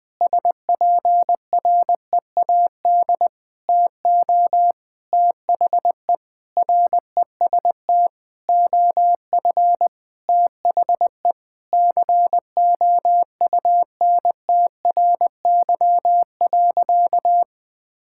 Morse code, heard as a keyed tone, about 20 wpm, 700 Hz. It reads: SPREAD TO THE REST OF THE COUNTRY.
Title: Sentences with 5-Word Phrases - 20wpm